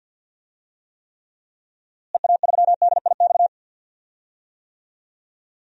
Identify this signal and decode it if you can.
Morse code, heard as a keyed tone, about 52 wpm, 700 Hz. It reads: EA3DIX
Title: DXCC-100 Call Signs - 6 Characters - Repeated 3x - 50wpm